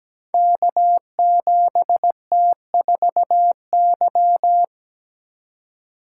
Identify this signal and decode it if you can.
Morse code, heard as a keyed tone, about 17 wpm, 700 Hz. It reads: K7T4Y
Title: Code Groups - 5 Letter:Numbers - 17wpm